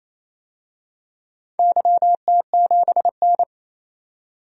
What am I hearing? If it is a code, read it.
Morse code, heard as a keyed tone, about 29 wpm, 700 Hz. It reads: YT7N
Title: DXCC-100 Call Signs - 4 Characters - 28wpm